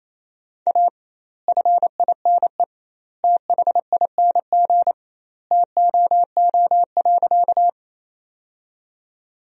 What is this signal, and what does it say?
Morse code, heard as a keyed tone, about 28 wpm, 700 Hz. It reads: A FINE THING TOO.
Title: 2x Word-Spacing - Sentences from Top 500 Words - 28wpm